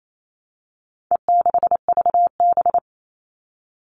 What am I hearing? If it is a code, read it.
Morse code, heard as a keyed tone, about 28 wpm, 700 Hz. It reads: E6VB
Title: All Cut Numbers - 4-Digit 28wpm